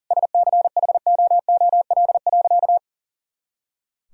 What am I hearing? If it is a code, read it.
Morse code, heard as a keyed tone, about 40 wpm, 700 Hz. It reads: SCHOOL.
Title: Sentences from Top 300 Words and Abbreviations 40wpm